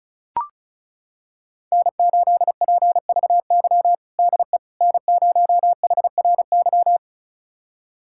Morse code, heard as a keyed tone, about 36 wpm, 700 Hz. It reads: N8PVY DE N0HRY
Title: USA Callsign DE Callsign - 5 Characters - 35wpm